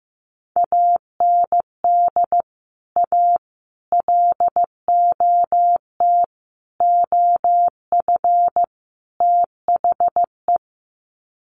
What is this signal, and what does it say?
Morse code, heard as a keyed tone, about 15 wpm, 700 Hz. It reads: AND A LOT OF THE
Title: Sets of 5 Words Encore 15wpm